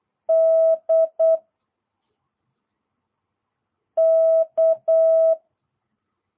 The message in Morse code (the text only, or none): D K